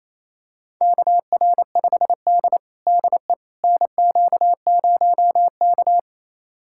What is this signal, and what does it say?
Morse code, heard as a keyed tone, about 27 wpm, 700 Hz. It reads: KR5D DE NQ0K
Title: USA Callsign DE Callsign - 4 Characters - 28wpm